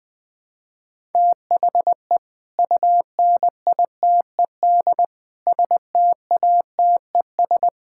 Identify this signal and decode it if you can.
Morse code, heard as a keyed tone, about 20 wpm, 700 Hz. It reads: THE UNITED STATES
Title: Build A Phrase - 5-Word Phrases - 20wpm